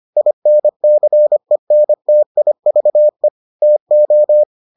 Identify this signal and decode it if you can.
Morse code, heard as a keyed tone, about 25 wpm, 600 Hz. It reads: INCENTIVE TO